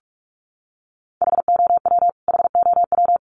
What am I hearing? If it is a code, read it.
Morse code, heard as a keyed tone, about 44 wpm, 700 Hz. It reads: HOW HOW
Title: Top 200 Words Repeated 45wpm